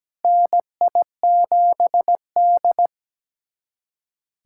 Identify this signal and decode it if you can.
Morse code, heard as a keyed tone, about 17 wpm, 700 Hz. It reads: NI7D
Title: USA Callsign DE Callsign - 4 Characters - 17wpm